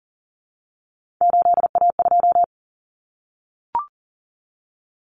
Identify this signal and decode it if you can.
Morse code, heard as a keyed tone, about 40 wpm, 700 Hz. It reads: 8A2
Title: Common Cut Numbers - 3-Digit 40wpm